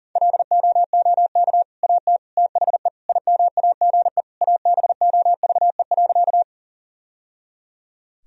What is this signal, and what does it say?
Morse code, heard as a keyed tone, about 40 wpm, 700 Hz. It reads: LOOK AT THE IMAGE ABOVE.
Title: Sentences from Top 1000 Words - Review Part 4 - 40wpm